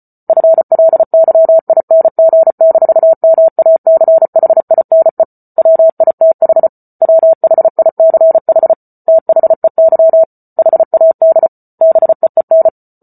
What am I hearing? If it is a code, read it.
Morse code, heard as a keyed tone, about 34 wpm, 650 Hz. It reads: FLYING-MACHINE WITH WHICH THEY HAD BEEN